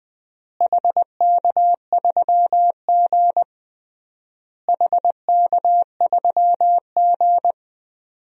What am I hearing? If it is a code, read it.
Morse code, heard as a keyed tone, about 20 wpm, 700 Hz. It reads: HK3G HK3G
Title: DXCC-100 Call Signs - 4 Characters - Repeated 3x - 20wpm